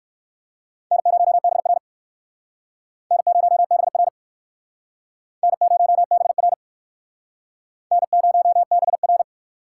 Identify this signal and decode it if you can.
Morse code, heard as a keyed tone, about 56 wpm, 700 Hz. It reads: N0BR N0BR N0BR N0BR
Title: Speed Racing - USA 1x2 Call Signs 45wpm